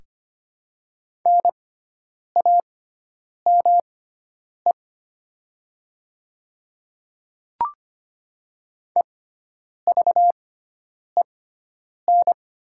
Morse code, heard as a keyed tone, about 25 wpm, spaced as for 9 wpm, 700 Hz. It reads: NAME EVEN
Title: 6x Character Spacing - Top 300 Words - 25wpm